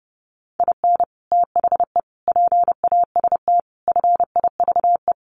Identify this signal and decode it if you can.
Morse code, heard as a keyed tone, about 28 wpm, 700 Hz. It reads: IN THE PAST FIVE
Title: Sets of 5 Words Encore 30wpm